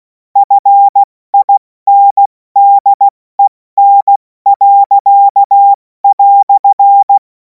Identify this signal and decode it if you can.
Morse code, heard as a keyed tone, about 16 wpm, spaced as for 14 wpm, 800 Hz. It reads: FINDEN."
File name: Morsekurs / Witze / Witz 03. Februar 2026 70 F80